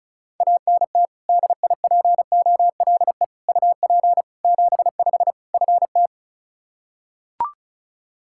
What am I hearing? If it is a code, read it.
Morse code, heard as a keyed tone, about 36 wpm, 700 Hz. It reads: ANT DIPOLE UP 75 FT
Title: Practice QSO - Exchange 2 - Info - 35wpm